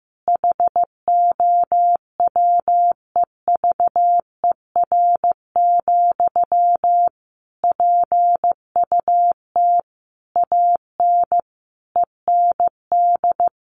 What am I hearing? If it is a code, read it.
Morse code, heard as a keyed tone, about 15 wpm, 700 Hz. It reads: HOWEVER, PUT AN END
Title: Sentences from Top 1200 Words 15wpm